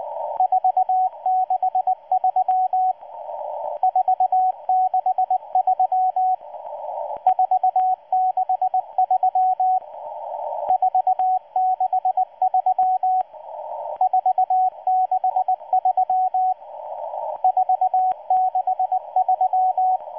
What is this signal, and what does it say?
Morse code, heard as a keyed tone, about 19 wpm, 740 Hz, with radio noise. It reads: 463 463 463 463 463 463